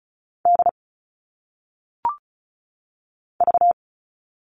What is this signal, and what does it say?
Morse code, heard as a keyed tone, about 36 wpm, 700 Hz. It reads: D V
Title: All Cut Numbers - 1-Digit 35wpm